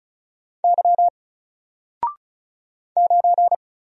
Morse code, heard as a keyed tone, about 36 wpm, 700 Hz. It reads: Y 9
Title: Single Letter-Number 35wpm